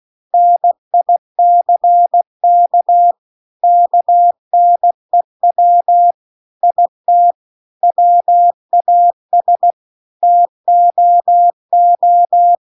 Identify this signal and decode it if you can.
Morse code, heard as a keyed tone, about 16 wpm, 700 Hz. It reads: NICK KNEW IT WAS TOO